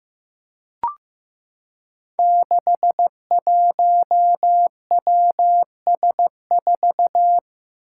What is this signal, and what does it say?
Morse code, heard as a keyed tone, about 15 wpm, 700 Hz. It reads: 61WS4
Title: Five Challenging Letter-Number Combinations 15wpm